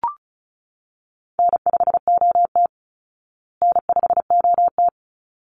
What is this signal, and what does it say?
Morse code, heard as a keyed tone, about 34 wpm, 700 Hz. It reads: N5OT N5OT
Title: Call Signs Repeated 3x - 35wpm